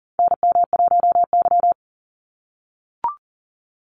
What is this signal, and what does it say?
Morse code, heard as a keyed tone, about 40 wpm, 700 Hz. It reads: NM1Y
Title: Call Signs - 40wpm